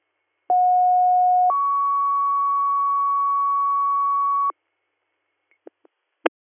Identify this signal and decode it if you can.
Two-tone page, heard as a one-second tone, then a three-second tone, about 730 then 1120 Hz. It is heard from 0.5 to 4.5 s.